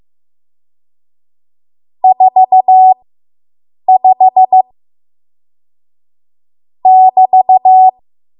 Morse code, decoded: 45 <BT>

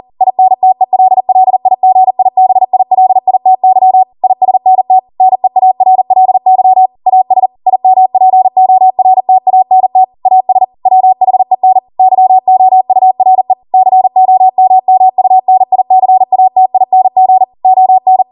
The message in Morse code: INTELLIGIBILITY ISNT NEARLY AS IMPORTANT AS WHEN YOURE COMMUNICATING ON

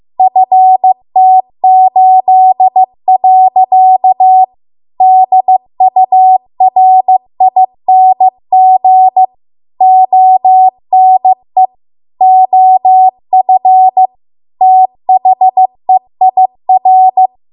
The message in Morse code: FT8. DURING ONE OF THEIR

